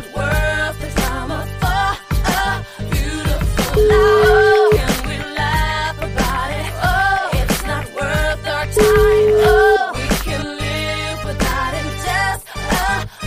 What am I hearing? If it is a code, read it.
Ringback tone (Europe, 425 Hz), with noise.